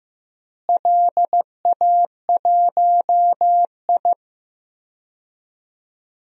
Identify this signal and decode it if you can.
Morse code, heard as a keyed tone, about 15 wpm, 700 Hz. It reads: LA1I
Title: DXCC-100 Call Signs - 4 Characters - Repeated 3x - 15wpm